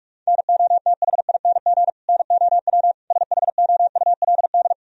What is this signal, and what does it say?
Morse code, heard as a keyed tone, about 46 wpm, 700 Hz. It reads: NOTHING NOW SHOULD
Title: Sentences from Top 1000 Words - Review Part 1 - 45wpm